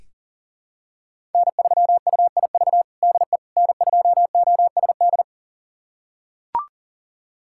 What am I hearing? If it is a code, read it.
Morse code, heard as a keyed tone, about 37 wpm, 700 Hz. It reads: N3UIV DE N2OSD
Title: USA Callsign DE Callsign - 5 Characters - 40wpm